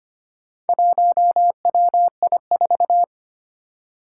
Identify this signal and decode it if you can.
Morse code, heard as a keyed tone, about 25 wpm, 700 Hz. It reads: 1WI4
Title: Four Challenging Letter-Number Combinations 25wpm